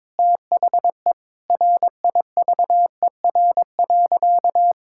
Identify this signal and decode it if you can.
Morse code, heard as a keyed tone, about 22 wpm, 700 Hz. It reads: THE RIVER.
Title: Sentences from Top 1000 Words - Review Part 6 - 22wpm